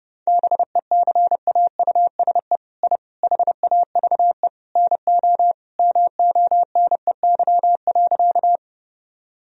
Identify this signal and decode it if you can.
Morse code, heard as a keyed tone, about 32 wpm, 700 Hz. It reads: BECAUSE I HAVE NO MONEY.
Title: Sentences from Top 1000 Words - Review Part 3 - 30wpm